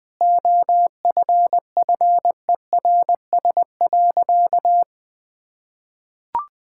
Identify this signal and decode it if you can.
Morse code, heard as a keyed tone, about 20 wpm, 700 Hz. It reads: OFFERS.
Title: Sentences with 6-Word Phrases - 20wpm